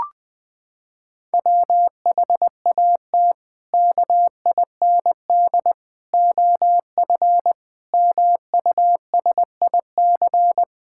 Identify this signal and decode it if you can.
Morse code, heard as a keyed tone, about 21 wpm, 700 Hz. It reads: WHAT KIND OF MUSIC
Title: Sentences with 3-Word Phrases - 20wpm